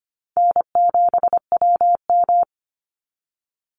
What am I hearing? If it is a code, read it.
Morse code, heard as a keyed tone, about 25 wpm, 700 Hz. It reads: N7WM
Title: NA SOTA Chaser Callsigns 25wpm